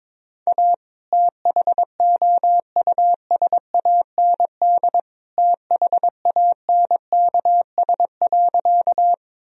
Morse code, heard as a keyed tone, about 22 wpm, 700 Hz. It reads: A THOUSAND THANKS.
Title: Sentences from Top 1000 Words - Review Part 3 - 22wpm